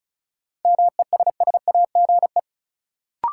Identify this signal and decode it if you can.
Morse code, heard as a keyed tone, about 34 wpm, 700 Hz. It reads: MESSAGE
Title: Sentences from Top 1100 Words - New Word Drill - 35wpm